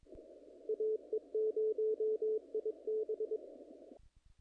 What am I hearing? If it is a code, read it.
Morse code, heard as a keyed tone, about 22 wpm, 430 Hz. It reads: AE0IB
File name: contests / naqp-cw-k5tr-op-k5pi- -1900